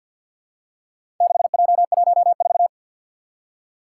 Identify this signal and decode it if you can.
Morse code, heard as a keyed tone, about 49 wpm, 700 Hz. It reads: 6J14
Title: Four Challenging Letter-Number Combinations 50wpm